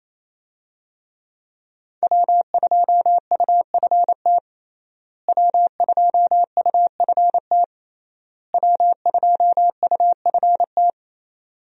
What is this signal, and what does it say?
Morse code, heard as a keyed tone, about 28 wpm, 700 Hz. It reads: W2UFT W2UFT W2UFT
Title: USA Call Signs - 5 Characters - Repeated 3x - 28wpm